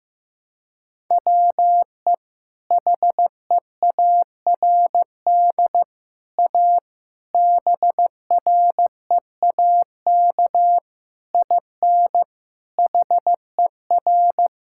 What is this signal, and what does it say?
Morse code, heard as a keyed tone, about 15 wpm, 700 Hz. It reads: WE HEARD A BREAK IN HER
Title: Sentences from Top 1000 Words - Review Part 6 - 15wpm